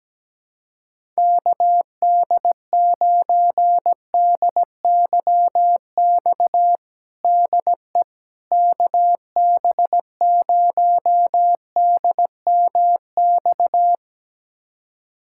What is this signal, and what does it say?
Morse code, heard as a keyed tone, about 17 wpm, 700 Hz. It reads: KD9DYX DE KB0DMX
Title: USA Callsign DE Callsign 17wpm